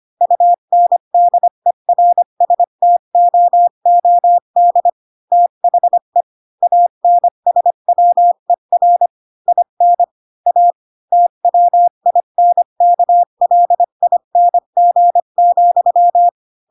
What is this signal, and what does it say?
Morse code, heard as a keyed tone, about 25 wpm, 700 Hz. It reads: UNDERSTOOD THE ANSWER IN A TWINKLING,
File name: Pinocchio / Pinocchio ch8 20 25 ik1hge